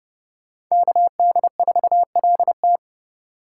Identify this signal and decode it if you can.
Morse code, heard as a keyed tone, about 31 wpm, 700 Hz. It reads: KD4LT
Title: Speed Racing - USA 2x2 Call Signs 25wpm